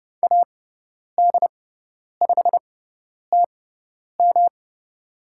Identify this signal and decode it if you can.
Morse code, heard as a keyed tone, about 30 wpm, spaced as for 10 wpm, 700 Hz. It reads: AD5TM
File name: Lesson 18g - Letter m - Call Signs 30wpm